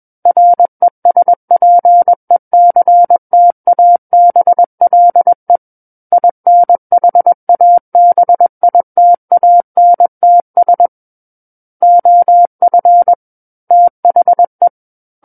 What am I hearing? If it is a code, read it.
Morse code, heard as a keyed tone, about 21 wpm, 700 Hz. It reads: RESPECTABLE INHABITANTS OF THE
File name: Book One / 12-Chapter 12 - 21 WPM